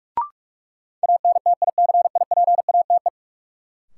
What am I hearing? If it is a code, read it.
Morse code, heard as a keyed tone, about 46 wpm, 700 Hz. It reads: ANTICIPATE
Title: Sentences from Top 1500 Words - New Word Drill - 45wpm